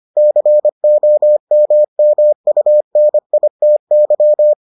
Morse code, heard as a keyed tone, about 25 wpm, 600 Hz. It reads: COMMUNITY